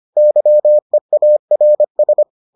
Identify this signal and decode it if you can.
Morse code, heard as a keyed tone, about 25 wpm, 600 Hz. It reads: YEARS